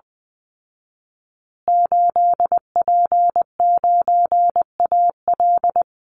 Morse code, heard as a keyed tone, about 19 wpm, 700 Hz. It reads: 8P9AL
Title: Call Signs Repeated 3x - 20wpm